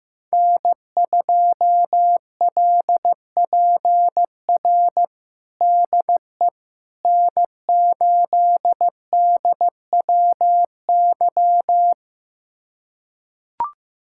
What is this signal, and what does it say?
Morse code, heard as a keyed tone, about 15 wpm, 700 Hz. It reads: N2LPR DE N8DWY
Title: USA Callsign DE Callsign - 5 Characters - 15wpm